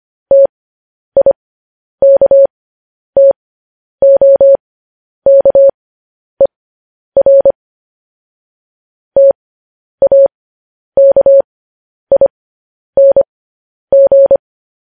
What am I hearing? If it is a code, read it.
Morse code, heard as a keyed tone, about 25 wpm, spaced as for 10 wpm, 550 Hz. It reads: TIKTOKER TAKING